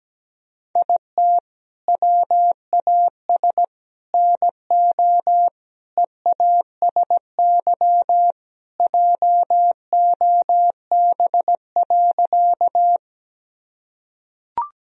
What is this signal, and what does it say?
Morse code, heard as a keyed tone, about 17 wpm, 700 Hz. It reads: IT WAS NO EASY JOB.
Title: Sentences from Top 1000 Words - Review Part 3 - 17wpm